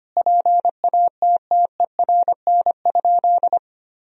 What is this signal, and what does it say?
Morse code, heard as a keyed tone, about 25 wpm, 700 Hz. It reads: PATTERN?
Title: Sentences from Top 1000 Words - Review Part 2 - 25wpm